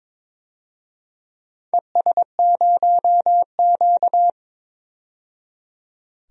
Morse code, heard as a keyed tone, about 22 wpm, 690 Hz. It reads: ES0Q
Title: DXCC-100 Call Signs - 4 Characters - 22wpm